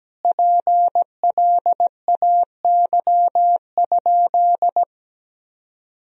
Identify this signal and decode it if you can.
Morse code, heard as a keyed tone, about 17 wpm, 700 Hz. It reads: PLAY?
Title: Sentences from Top 300 Words and Abbreviations 17wpm